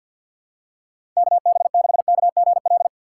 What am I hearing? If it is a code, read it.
Morse code, heard as a keyed tone, about 51 wpm, 700 Hz. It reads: KB6KGL
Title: USA Call Signs - 6 Characters - 4x Character Spacing - 50wpm